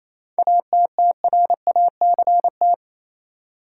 Morse code, heard as a keyed tone, about 28 wpm, 700 Hz. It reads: ATTRACT
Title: Seven Letter Words - 28wpm